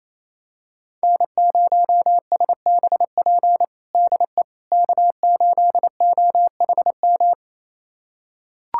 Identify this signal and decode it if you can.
Morse code, heard as a keyed tone, about 28 wpm, 700 Hz. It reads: N0SBP DE K8OHM
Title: USA Callsign DE Callsign - 5 Characters - 28wpm